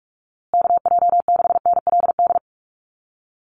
Morse code, heard as a keyed tone, about 44 wpm, 700 Hz. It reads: KJ6NLD
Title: USA Call Signs Repeated 2x 45wpm